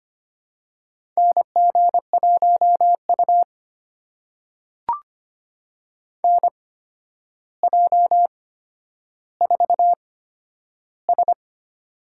Morse code, heard as a keyed tone, about 25 wpm, spaced as for 11 wpm, 700 Hz. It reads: NG1U NJ4S
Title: USA Call Signs - 4 Characters - 8x Character Spacing - 25wpm